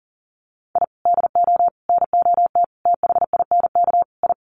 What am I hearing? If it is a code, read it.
Morse code, heard as a keyed tone, about 40 wpm, 700 Hz. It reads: I DO NOT THINK I
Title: Sets of 5 Words 40wpm